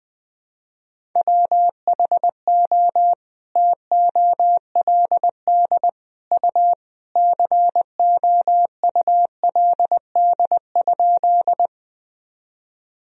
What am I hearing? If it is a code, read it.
Morse code, heard as a keyed tone, about 21 wpm, 700 Hz. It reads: WHO TOLD U COULD?
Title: Sentences from Top 400 Words and Abbreviations 20wpm